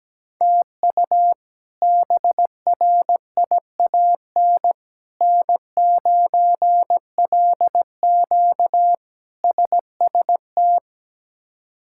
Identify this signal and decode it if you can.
Morse code, heard as a keyed tone, about 17 wpm, 700 Hz. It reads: TU BRIAN N9LQ SST
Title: K1USN SST QSO Practice - 17wpm